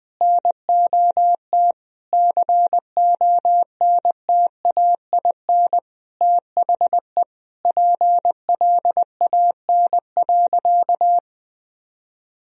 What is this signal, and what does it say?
Morse code, heard as a keyed tone, about 20 wpm, 700 Hz. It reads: NOT CONTAIN THE PLAN.
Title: Sentences from Top 1000 Words - Review Part 2 - 20wpm